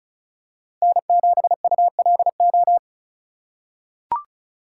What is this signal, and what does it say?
Morse code, heard as a keyed tone, about 36 wpm, 700 Hz. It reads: N7ULO
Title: USA Call Signs - 5 Characters - 3x Character Spacing - 35wpm